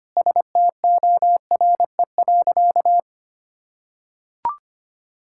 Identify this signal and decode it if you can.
Morse code, heard as a keyed tone, about 25 wpm, 700 Hz. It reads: STORE.
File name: practice / 25wpm / Sentences with 9-Word Phrases - 25wpm